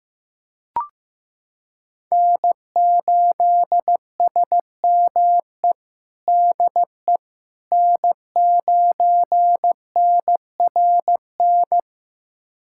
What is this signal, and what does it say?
Morse code, heard as a keyed tone, about 15 wpm, 700 Hz. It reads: N8SME DE N9NRN
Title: USA Callsign DE Callsign - 5 Characters - 15wpm